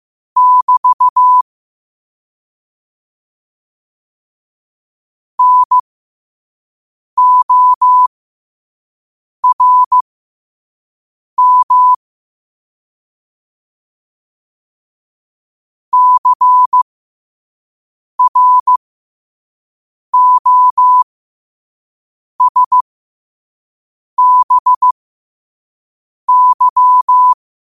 Morse code, heard as a keyed tone, about 15 wpm, spaced as for 5 wpm, 1000 Hz. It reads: <BT> NORM CROSBY
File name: qotdfiles / 5wpm